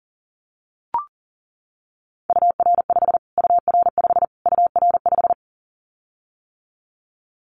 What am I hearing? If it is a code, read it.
Morse code, heard as a keyed tone, about 40 wpm, 700 Hz. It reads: UR5 UR5 UR5
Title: Callsign Prefixes for Top 25 DXCC - Repeated 3x - 40wpm